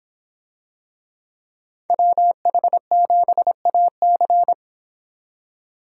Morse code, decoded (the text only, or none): WH7AC